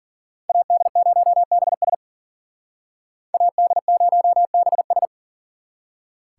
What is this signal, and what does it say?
Morse code, heard as a keyed tone, about 51 wpm, 700 Hz. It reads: AD0BS AD0BS